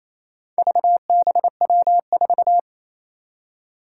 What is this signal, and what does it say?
Morse code, heard as a keyed tone, about 28 wpm, 700 Hz. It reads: VBW4
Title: Four Challenging Letter-Number Combinations 28wpm